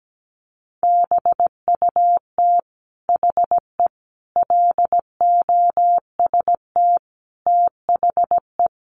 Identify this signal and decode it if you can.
Morse code, heard as a keyed tone, about 17 wpm, 700 Hz. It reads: BUT HE LOST THE
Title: Sentences from Top 1500 Words 17wpm